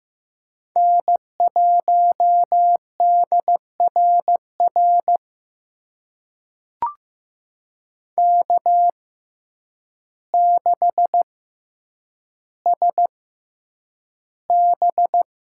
Morse code, heard as a keyed tone, about 15 wpm, spaced as for 8 wpm, 700 Hz. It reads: N1DRR K6SB
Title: USA Call Signs - 6x Character Spacing - 15wpm